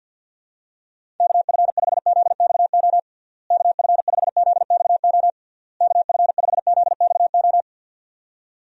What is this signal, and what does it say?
Morse code, heard as a keyed tone, about 48 wpm, 700 Hz. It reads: KF5ZXO KF5ZXO KF5ZXO
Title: USA Call Signs - 6 Characters - Repeated 3x - 50wpm